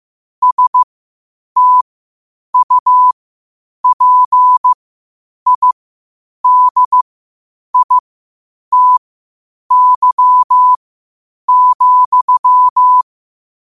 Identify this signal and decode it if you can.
Morse code, heard as a keyed tone, about 15 wpm, spaced as for 8 wpm, 1000 Hz. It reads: STUPIDITY,
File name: qotdfiles / 8wpm